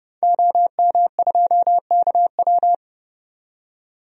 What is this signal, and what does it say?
Morse code, heard as a keyed tone, about 30 wpm, 700 Hz. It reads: OM2KW
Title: DXCC-100 Call Signs - 5 Characters - 30wpm